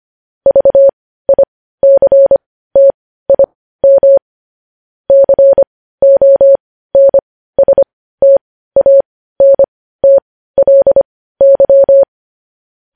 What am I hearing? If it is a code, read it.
Morse code, heard as a keyed tone, about 25 wpm, spaced as for 15 wpm, 550 Hz. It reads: VICTIM CONSTANTLY
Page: news headlines in morse code at 15 wpm